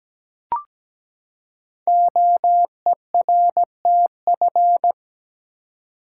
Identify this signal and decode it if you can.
Morse code, heard as a keyed tone, about 17 wpm, 700 Hz. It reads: OERTF
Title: Code Groups - 5 Letters - 17wpm